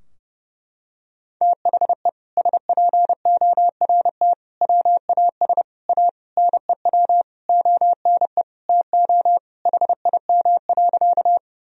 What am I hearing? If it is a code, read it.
Morse code, heard as a keyed tone, about 30 wpm, 700 Hz. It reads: THE SPORT WAS A NEW ONE TO HIM.
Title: Sentences from Top 1000 Words - Review Part 4 - 30wpm